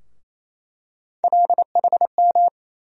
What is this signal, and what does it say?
Morse code, heard as a keyed tone, about 27 wpm, 700 Hz. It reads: LHM
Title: USA Call Sign Postfixes - 3 Letters 28wpm